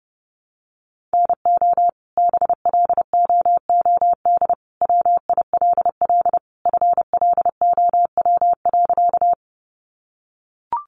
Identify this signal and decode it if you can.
Morse code, heard as a keyed tone, about 29 wpm, 700 Hz. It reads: NO BLOOD WILL FLOW.
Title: Sentences from Top 1200 Words 30wpm